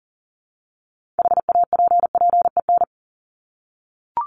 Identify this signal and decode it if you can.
Morse code, heard as a keyed tone, about 40 wpm, 700 Hz. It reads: HAPPEN
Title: ED Suffix - Top 100 Words with Base Words 40wpm